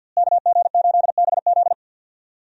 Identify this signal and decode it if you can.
Morse code, heard as a keyed tone, about 46 wpm, 700 Hz. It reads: KG8BZ
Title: USA Call Signs - 5 Characters - Repeated 2x - 50wpm